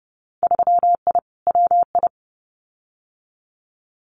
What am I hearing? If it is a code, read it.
Morse code, heard as a keyed tone, about 31 wpm, 700 Hz. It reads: 3I WI
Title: Winter Field Day Exchanges 30wpm